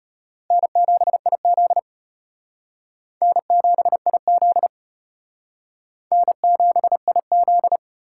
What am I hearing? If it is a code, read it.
Morse code, heard as a keyed tone, about 34 wpm, 700 Hz. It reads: N7IZ N7IZ N7IZ